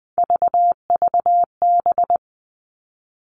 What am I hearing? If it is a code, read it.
Morse code, heard as a keyed tone, about 20 wpm, 700 Hz. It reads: VVB
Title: Three Challenging Letter-Number Combinations 20wpm